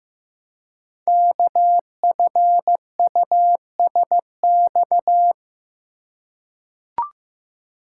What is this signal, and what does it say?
Morse code, heard as a keyed tone, about 15 wpm, 700 Hz. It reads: KFUSX
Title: Code Groups - 5 Letters - 15wpm